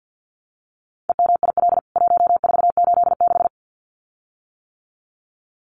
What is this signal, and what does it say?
Morse code, heard as a keyed tone, about 48 wpm, 700 Hz. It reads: EMIL 1486
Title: CWT Exchange 50wpm